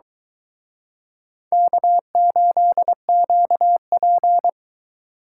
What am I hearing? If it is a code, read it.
Morse code, heard as a keyed tone, about 23 wpm, 700 Hz. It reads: K8QP